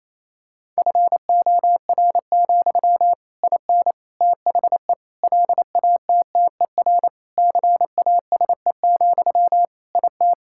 Morse code, heard as a keyed tone, about 27 wpm, 700 Hz. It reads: FOR, IN THE LATTER CASE, IT
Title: Sentences from Top 1600 Words 28wpm